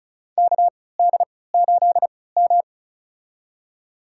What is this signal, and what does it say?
Morse code, heard as a keyed tone, about 34 wpm, spaced as for 20 wpm, 700 Hz. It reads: KD8M
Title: USA Call Signs - 4 Characters - 3x Character Spacing - 35wpm